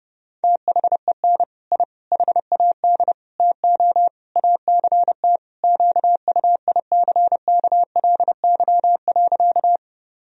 Morse code, heard as a keyed tone, about 30 wpm, 700 Hz. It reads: THEN I HAD TO ACT QUICKLY.
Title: Sentences from Top 1000 Words - Review Part 1 - 30wpm